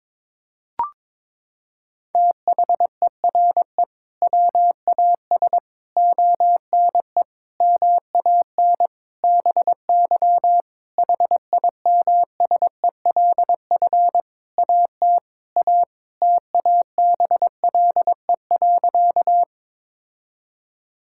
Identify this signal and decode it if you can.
Morse code, heard as a keyed tone, about 22 wpm, 700 Hz. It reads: THERE WAS ONE MAN BY HIMSELF AT A TABLE.
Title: Sentences from Top 1600 Words 22wpm